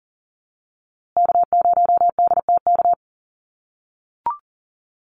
Morse code, heard as a keyed tone, about 40 wpm, 700 Hz. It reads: K0DTK
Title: USA Call Signs Repeated 3x 40wpm